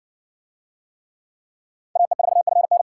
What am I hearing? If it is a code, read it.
Morse code, heard as a keyed tone, about 60 wpm, 700 Hz. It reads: AE4FN